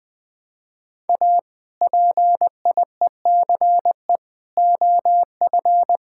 Morse code, heard as a keyed tone, about 20 wpm, 700 Hz. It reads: A PIECE OF
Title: Build A Phrase - 3-Word Phrases - 20wpm